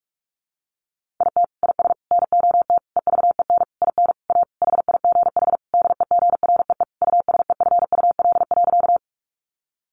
Morse code, heard as a keyed tone, about 44 wpm, 700 Hz. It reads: IT IS NOT EVEN IN A HIGH DEGREE USEFUL.
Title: Sentences from Top 1500 Words 45wpm